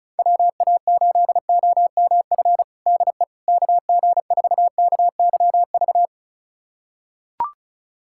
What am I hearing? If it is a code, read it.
Morse code, heard as a keyed tone, about 36 wpm, 700 Hz. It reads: WA8OMF DE KG4KYV
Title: USA Callsign DE Callsign - 6 Characters - 35wpm